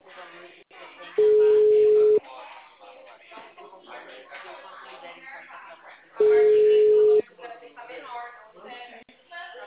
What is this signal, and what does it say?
Ringback tone (Europe, 425 Hz), with speech.